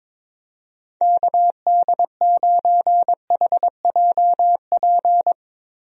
Morse code, heard as a keyed tone, about 22 wpm, 700 Hz. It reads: KD9HJP
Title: USA Call Signs - 6 Characters - Repeated 2x - 22wpm